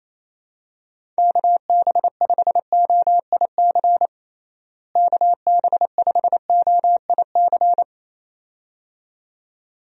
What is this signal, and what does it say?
Morse code, heard as a keyed tone, about 28 wpm, 700 Hz. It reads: KB5OIC KB5OIC
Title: USA Call Signs - 6 Characters - Repeated 2x - 28wpm